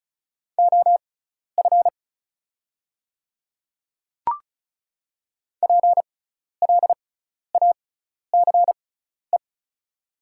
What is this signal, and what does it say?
Morse code, heard as a keyed tone, about 34 wpm, spaced as for 12 wpm, 700 Hz. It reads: OF PLACE
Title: 6x Character Spacing - Top 200 Words - 35wpm